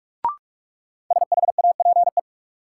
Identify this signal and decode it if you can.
Morse code, heard as a keyed tone, about 41 wpm, 700 Hz. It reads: SHAPE
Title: Sentences from Top 500 Words - New Word Drill - 45wpm